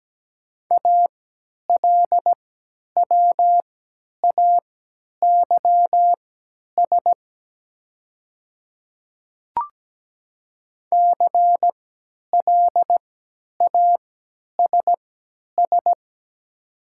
Morse code, heard as a keyed tone, about 17 wpm, spaced as for 10 wpm, 700 Hz. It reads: ALWAYS CLASS
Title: 3x Character Spacing - Top 500 Words - 17wpm